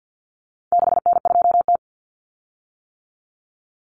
Morse code, heard as a keyed tone, about 50 wpm, 700 Hz. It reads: 6N2T
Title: Common Cut Numbers - 4-Digit 50wpm